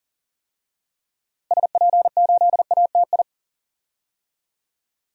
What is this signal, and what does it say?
Morse code, heard as a keyed tone, about 40 wpm, 700 Hz. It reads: SP8ATI
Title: DXCC-100 Call Signs - 6 Characters - Repeated 3x - 40wpm